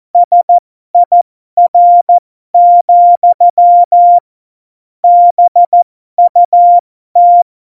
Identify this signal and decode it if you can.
Morse code, heard as a keyed tone, about 14 wpm, spaced as for 12 wpm, 700 Hz. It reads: SIR, BUT
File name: Morsekurs / Witze